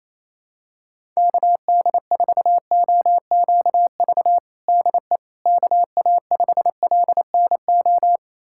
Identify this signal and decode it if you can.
Morse code, heard as a keyed tone, about 27 wpm, 700 Hz. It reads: KD4OQV DE KA5LNO
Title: USA Callsign DE Callsign - 6 Characters - 28wpm